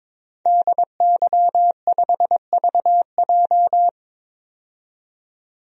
Morse code, heard as a keyed tone, about 22 wpm, 700 Hz. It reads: DY5VJ
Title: DXCC-100 Call Signs - 5 Characters - 22wpm